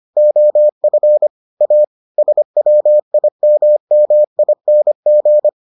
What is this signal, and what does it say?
Morse code, heard as a keyed tone, about 25 wpm, 600 Hz. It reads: OF A SWIMMING